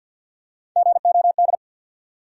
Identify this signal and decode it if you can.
Morse code, heard as a keyed tone, about 49 wpm, 700 Hz. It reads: GOD
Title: Three Letter Words - 50wpm